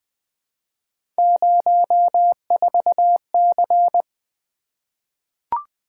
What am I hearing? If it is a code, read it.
Morse code, heard as a keyed tone, about 20 wpm, 700 Hz. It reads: 04C
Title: Three Letter-Number Combinations - 20wpm